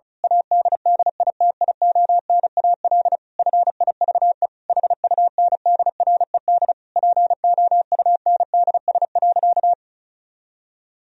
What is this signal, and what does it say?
Morse code, heard as a keyed tone, about 36 wpm, 700 Hz. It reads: ADDITIONAL FIVE HUNDRED POUNDS.
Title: Sentences from Top 1000 Words - Review Part 4 - 35wpm